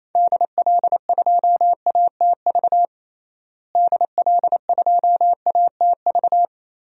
Morse code, heard as a keyed tone, about 28 wpm, 700 Hz. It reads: DL2ATV DL2ATV
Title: DXCC-100 Call Signs - 6 Characters - Repeated 3x - 28wpm